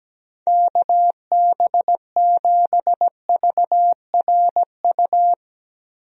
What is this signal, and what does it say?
Morse code, heard as a keyed tone, about 17 wpm, 700 Hz. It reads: KB7VRU
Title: USA Call Signs - 6 Characters - Repeated 2x - 17wpm